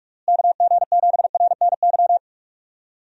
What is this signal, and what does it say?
Morse code, heard as a keyed tone, about 46 wpm, 700 Hz. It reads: KG7RNY
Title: USA Call Signs - 6 Characters - Repeated 3x - 45wpm